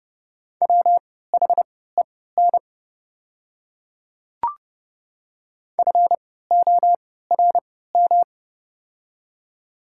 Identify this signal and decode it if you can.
Morse code, heard as a keyed tone, about 28 wpm, spaced as for 17 wpm, 690 Hz. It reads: WHEN FORM
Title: 3x Character Spacing - Top 300 Words - 30wpm